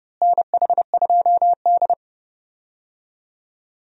Morse code, heard as a keyed tone, about 30 wpm, 700 Hz. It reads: NH2D
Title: USA Call Signs - 4 Characters - Repeated 3x - 30wpm